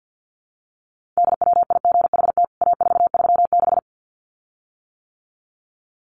Morse code, heard as a keyed tone, about 48 wpm, 700 Hz. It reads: DWIGHT A436
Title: CWT Exchange 50wpm